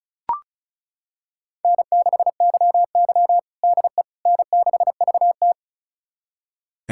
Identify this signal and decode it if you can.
Morse code, heard as a keyed tone, about 36 wpm, 700 Hz. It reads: N6YY DE N6VT
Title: USA Callsign DE Callsign - 4 Characters - 35wpm